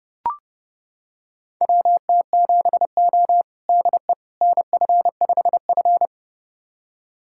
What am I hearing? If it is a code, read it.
Morse code, heard as a keyed tone, about 32 wpm, 700 Hz. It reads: WT7O DE NF5F
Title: USA Callsign DE Callsign - 4 Characters - 30wpm